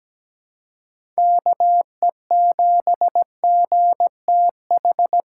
Morse code, heard as a keyed tone, about 17 wpm, 700 Hz. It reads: KE7GTH
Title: USA Call Signs - 6 Characters - 3x Character Spacing - 17wpm